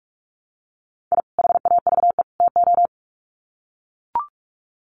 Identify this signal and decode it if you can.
Morse code, heard as a keyed tone, about 44 wpm, 700 Hz. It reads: I HAVE TO
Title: Sets of 3 Words 45wpm